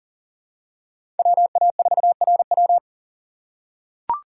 Morse code, heard as a keyed tone, about 40 wpm, 700 Hz. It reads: WA4RW
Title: USA Call Signs - 5 Characters - 4x Character Spacing - 40wpm